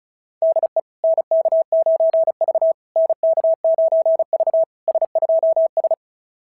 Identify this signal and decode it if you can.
Morse code, heard as a keyed tone, about 36 wpm, 690 Hz. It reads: DE NK9V NK9V S2S
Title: SOTA QSO Practice 35wpm